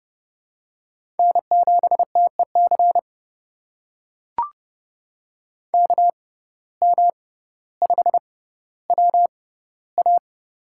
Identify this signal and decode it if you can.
Morse code, heard as a keyed tone, about 30 wpm, spaced as for 15 wpm, 700 Hz. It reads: N7TEC KM5WA
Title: USA Call Signs - 5 Characters - 6x Character Spacing - 30wpm